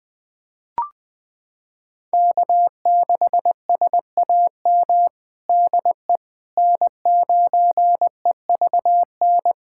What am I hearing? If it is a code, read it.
Morse code, heard as a keyed tone, about 21 wpm, 700 Hz. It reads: K6SAM DE N9EVN
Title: USA Callsign DE Callsign - 5 Characters - 20wpm